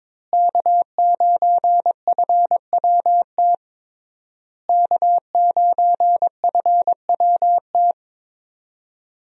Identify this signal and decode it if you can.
Morse code, heard as a keyed tone, about 22 wpm, 700 Hz. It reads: K9FWT K9FWT
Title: USA Call Signs - 5 Characters - Repeated 3x - 22wpm